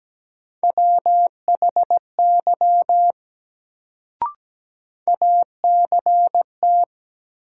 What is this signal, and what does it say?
Morse code, heard as a keyed tone, about 17 wpm, 700 Hz. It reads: WHY ACT
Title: Sentences from Top 200 Words - New Word Drill - 17wpm